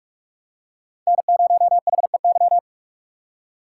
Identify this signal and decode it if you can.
Morse code, heard as a keyed tone, about 44 wpm, 700 Hz. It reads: N0HEY
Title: USA Call Signs - 5 Characters - 6x Character Spacing - 45wpm